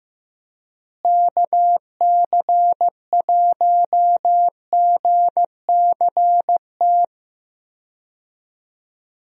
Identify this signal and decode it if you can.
Morse code, heard as a keyed tone, about 15 wpm, 700 Hz. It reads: KC1GCT
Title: USA Call Signs - 6 Characters - Repeated 3x - 15wpm